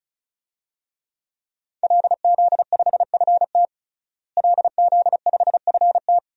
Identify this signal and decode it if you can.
Morse code, heard as a keyed tone, about 36 wpm, 700 Hz. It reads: LZ5FT LZ5FT
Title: DXCC-100 Call Signs - 5 Characters - Repeated 2x - 35wpm